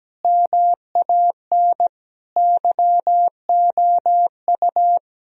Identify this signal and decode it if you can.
Morse code, heard as a keyed tone, about 17 wpm, 700 Hz. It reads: MAN YOU
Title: Sentences from Top 1000 Words - Review Part 3 - 17wpm